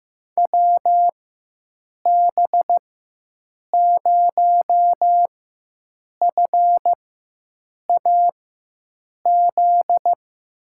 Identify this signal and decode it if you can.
Morse code, heard as a keyed tone, about 15 wpm, spaced as for 7 wpm, 700 Hz. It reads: WB0FAZ
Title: USA Call Signs - 6 Characters - 4x Character Spacing - 15wpm